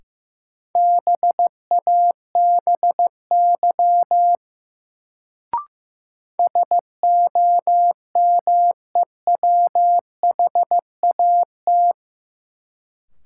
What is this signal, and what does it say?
Morse code, heard as a keyed tone, about 16 wpm, 700 Hz. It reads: BABY SOMEWHAT
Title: Sentences from Top 1300 Words - New Word Drill Alt - 15wpm